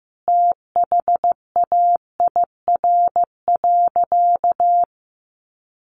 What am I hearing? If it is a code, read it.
Morse code, heard as a keyed tone, about 14 wpm, 700 Hz. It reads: THAIR.
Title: A Sound Of Thunder 15wpm